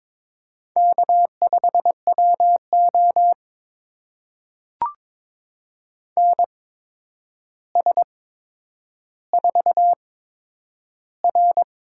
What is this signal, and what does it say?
Morse code, heard as a keyed tone, about 22 wpm, spaced as for 9 wpm, 700 Hz. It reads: K5WO NS4R
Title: USA Call Signs - 4 Characters - 8x Character Spacing - 22wpm